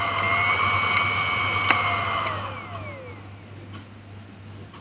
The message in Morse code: none